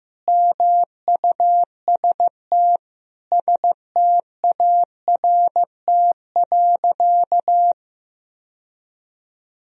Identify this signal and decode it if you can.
Morse code, heard as a keyed tone, about 15 wpm, 700 Hz. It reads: MUST START.
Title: Sentences from Top 1000 Words - Review Part 1 - 15wpm